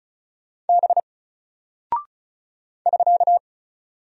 Morse code, heard as a keyed tone, about 34 wpm, 700 Hz. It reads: B <SK>
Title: Single Character (Letter:Number:Punctuation:Prosign) 35wpm